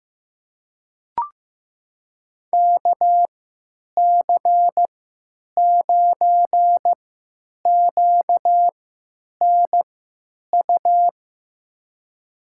Morse code, heard as a keyed tone, about 16 wpm, spaced as for 9 wpm, 700 Hz. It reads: KC9QNU
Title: USA Call Signs - 6 Characters - 3x Character Spacing - 15wpm